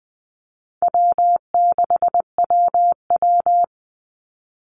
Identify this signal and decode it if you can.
Morse code, heard as a keyed tone, about 20 wpm, 700 Hz. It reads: W6WW
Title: NA SOTA Chaser Callsigns 20wpm